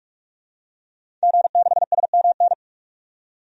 Morse code, heard as a keyed tone, about 44 wpm, 700 Hz. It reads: G6SMN
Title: DXCC-100 Call Signs - 5 Characters - 45wpm